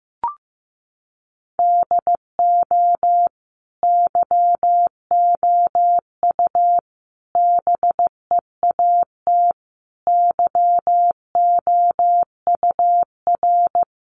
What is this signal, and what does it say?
Morse code, heard as a keyed tone, about 14 wpm, 700 Hz. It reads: DO YOU BEAT YOUR
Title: Sentences from Top 1300 Words 15wpm